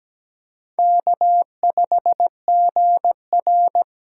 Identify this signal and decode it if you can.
Morse code, heard as a keyed tone, about 17 wpm, 700 Hz. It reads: K5GR
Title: USA Call Signs - 4 Characters - 4x Character Spacing - 17wpm